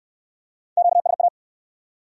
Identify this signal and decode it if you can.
Morse code, heard as a keyed tone, about 63 wpm, 700 Hz. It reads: QSA